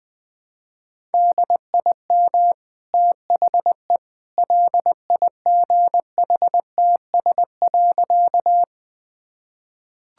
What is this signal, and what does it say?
Morse code, heard as a keyed tone, about 20 wpm, 700 Hz. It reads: DIM THE LIGHTS.
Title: Easy Sentences with Hints - 20wpm